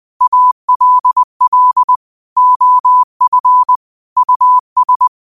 Morse code, heard as a keyed tone, about 20 wpm, 1000 Hz. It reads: ALL OF US
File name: qotdfiles / 20wpm